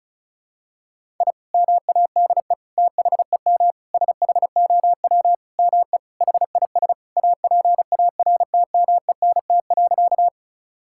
Morse code, heard as a keyed tone, about 34 wpm, 700 Hz. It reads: I MADE THEM SHOW ME HIS APARTMENT.
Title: Sentences from Top 1000 Words - Review Part 6 - 35wpm